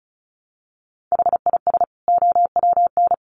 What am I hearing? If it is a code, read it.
Morse code, heard as a keyed tone, about 34 wpm, 700 Hz. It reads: HIS OWN
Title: Sets of 2 Words 35wpm